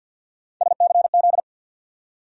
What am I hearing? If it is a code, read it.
Morse code, heard as a keyed tone, about 51 wpm, 700 Hz. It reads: SCZ
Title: USA Call Sign Postfixes - 3 Letters 50wpm